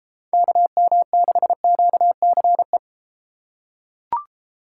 Morse code, heard as a keyed tone, about 32 wpm, 700 Hz. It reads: KM6QCE